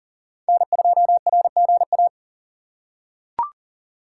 Morse code, heard as a keyed tone, about 40 wpm, 700 Hz. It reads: N2RGA
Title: USA Call Signs - 5 Characters - 2x Character Spacing - 40wpm